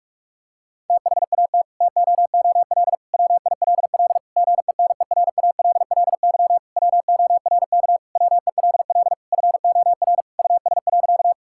Morse code, heard as a keyed tone, about 46 wpm, 700 Hz. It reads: THAT TOOL WILL GENERALLY WORK WELL FOR US.
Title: Sentences from Top 1000 Words - Review Part 6 - 45wpm